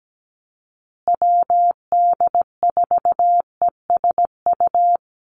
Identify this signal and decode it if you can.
Morse code, heard as a keyed tone, about 17 wpm, 700 Hz. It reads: WD4ESU
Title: USA Call Signs Repeated 3x 17wpm